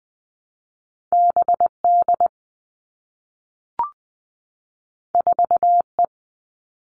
Morse code, heard as a keyed tone, about 20 wpm, 700 Hz. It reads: BD 4E
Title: All Cut Numbers - 2-Digit 20wpm